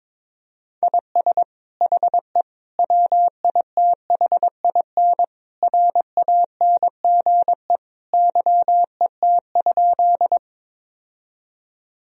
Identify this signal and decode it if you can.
Morse code, heard as a keyed tone, about 22 wpm, 700 Hz. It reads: IS HE WITHIN RANGE YET?
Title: Sentences from Top 1000 Words - Review Part 4 - 22wpm